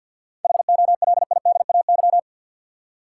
Morse code, heard as a keyed tone, about 47 wpm, 700 Hz. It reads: HOLIDAY
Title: Seven Letter Words - 50wpm